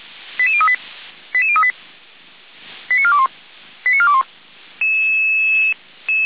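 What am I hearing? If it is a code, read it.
Five-tone selective call (ZVEI), heard as quick runs of five tones, tones of about 70 ms. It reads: 88038 88038 89421 89421
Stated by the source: ZVEI five-tone call received on an SDR.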